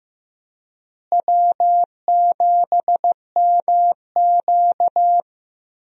Morse code, heard as a keyed tone, about 14 wpm, 700 Hz. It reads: W7MQ
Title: USA Call Signs - 4 Characters - Repeated 3x - 15wpm